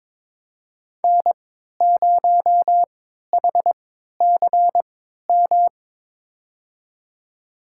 Morse code, heard as a keyed tone, about 22 wpm, spaced as for 13 wpm, 700 Hz. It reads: N0HCM
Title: USA Call Signs - 5 Characters - 3x Character Spacing - 22wpm